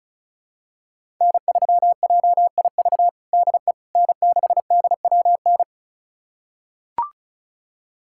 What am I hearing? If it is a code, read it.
Morse code, heard as a keyed tone, about 36 wpm, 700 Hz. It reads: N3JIV DE N6DWN
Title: USA Callsign DE Callsign - 5 Characters - 35wpm